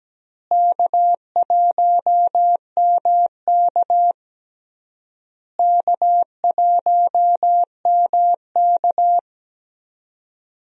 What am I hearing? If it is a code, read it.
Morse code, heard as a keyed tone, about 17 wpm, 700 Hz. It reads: K1MK K1MK
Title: USA Call Signs - 4 Characters - Repeated 3x - 17wpm